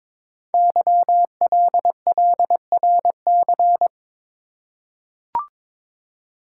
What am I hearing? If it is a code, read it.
Morse code, heard as a keyed tone, about 22 wpm, 700 Hz. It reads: YLLRC
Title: Code Groups - 5 Letters - 22wpm